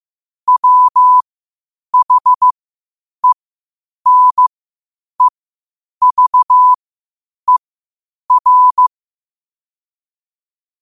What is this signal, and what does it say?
Morse code, heard as a keyed tone, about 15 wpm, spaced as for 8 wpm, 1000 Hz. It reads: WHENEVER
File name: 8wpm